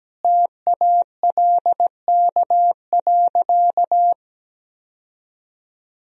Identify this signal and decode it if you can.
Morse code, heard as a keyed tone, about 17 wpm, 700 Hz. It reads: TALK.
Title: Sentences from Top 1000 Words - Review Part 1 - 17wpm